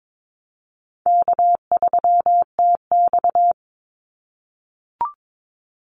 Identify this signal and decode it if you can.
Morse code, heard as a keyed tone, about 22 wpm, 700 Hz. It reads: K3TX
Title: Call Signs - 22wpm